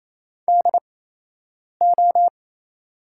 Morse code, heard as a keyed tone, about 28 wpm, spaced as for 8 wpm, 700 Hz. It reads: DO
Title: 8x Character Spacing - Top 200 Words - 28wpm